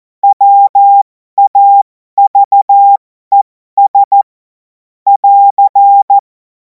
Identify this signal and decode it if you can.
Morse code, heard as a keyed tone, about 14 wpm, spaced as for 12 wpm, 800 Hz. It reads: WAVES <AR>
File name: Morsekurs / Witze